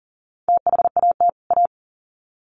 Morse code, heard as a keyed tone, about 40 wpm, 700 Hz. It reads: THAT A
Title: Sets of 2 Words 40wpm